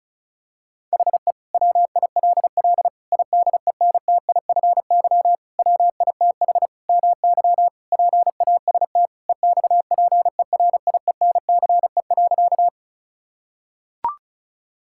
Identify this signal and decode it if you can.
Morse code, heard as a keyed tone, about 36 wpm, 700 Hz. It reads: HE WILL IDENTIFY WITH MY PAST EXPERIENCE.
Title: Sentences from Top 1000 Words - Review Part 6 - 35wpm